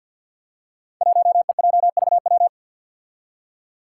Morse code, heard as a keyed tone, about 47 wpm, 700 Hz. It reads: 1EJVW
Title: Five Challenging Letter-Number Combinations 50wpm